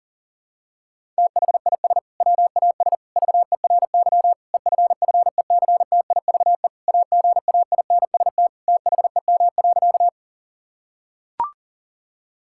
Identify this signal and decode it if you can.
Morse code, heard as a keyed tone, about 40 wpm, 700 Hz. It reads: THIS WAS VERY EFFECTIVE AGAINST THEM.
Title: Sentences from Top 1000 Words - Review Part 6 - 40wpm